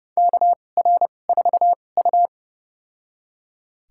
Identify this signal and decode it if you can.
Morse code, heard as a keyed tone, about 30 wpm, spaced as for 22 wpm, 700 Hz. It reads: KR4U
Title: USA Call Signs - 4 Characters - 2x Character Spacing - 30wpm